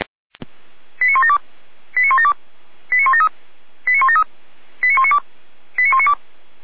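Five-tone selective call (ZVEI), heard as quick runs of five tones, tones of about 70 ms. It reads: 89172 89172 89173 89173 89192 89192
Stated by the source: Received on SDR; ZVEI five-tone selective call.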